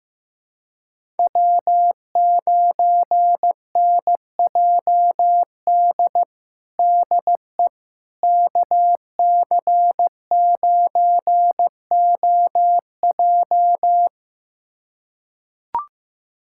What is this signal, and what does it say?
Morse code, heard as a keyed tone, about 15 wpm, 700 Hz. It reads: W9NJD DE KC9OJ
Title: USA Callsign DE Callsign - 5 Characters - 15wpm